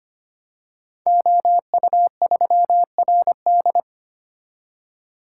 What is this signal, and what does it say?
Morse code, heard as a keyed tone, about 25 wpm, 700 Hz. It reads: OU3RD
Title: Code Groups - 5 Letter:Numbers - 25wpm